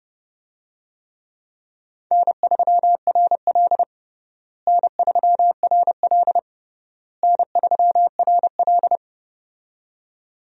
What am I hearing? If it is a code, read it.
Morse code, heard as a keyed tone, about 30 wpm, 700 Hz. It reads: N3RL N3RL N3RL
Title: USA Call Signs - 4 Characters - Repeated 3x - 30wpm